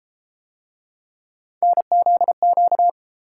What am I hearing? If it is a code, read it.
Morse code, heard as a keyed tone, about 31 wpm, 700 Hz. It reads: NZQ